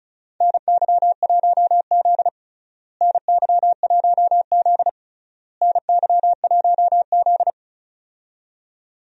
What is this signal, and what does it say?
Morse code, heard as a keyed tone, about 34 wpm, 700 Hz. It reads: NY1Z NY1Z NY1Z
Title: USA Call Signs - 4 Characters - Repeated 3x - 35wpm